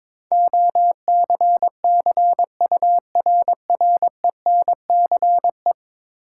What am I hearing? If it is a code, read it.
Morse code, heard as a keyed tone, about 22 wpm, 700 Hz. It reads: OCCURRENCE
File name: practice / 22wpm / 10 Letter Words - First 3 Letters Given - 22wpm